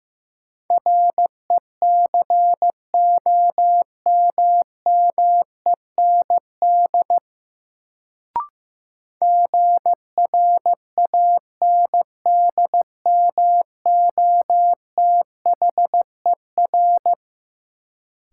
Morse code, heard as a keyed tone, about 15 wpm, 700 Hz. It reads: RECOMMEND GRANDMOTHER
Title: Sentences from Top 1200 Words - New Word Drill - 15wpm